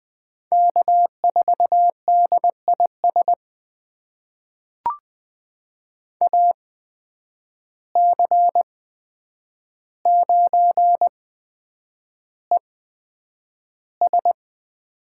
Morse code, heard as a keyed tone, about 20 wpm, spaced as for 9 wpm, 700 Hz. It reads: K4DIS AC9ES
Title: USA Call Signs - 8x Character Spacing - 20wpm